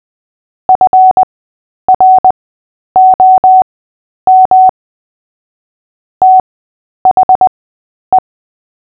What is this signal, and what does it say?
Morse code, heard as a keyed tone, about 20 wpm, spaced as for 10 wpm, 750 Hz. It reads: FROM THE
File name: Short-Story-7-10wpm